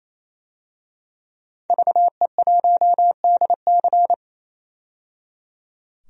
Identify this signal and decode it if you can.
Morse code, heard as a keyed tone, about 27 wpm, 700 Hz. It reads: VE1DC
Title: DXCC-100 Call Signs - 5 Characters - 28wpm